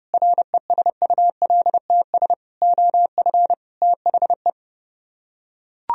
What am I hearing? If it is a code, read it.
Morse code, heard as a keyed tone, about 30 wpm, 700 Hz. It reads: RESULTS OF THE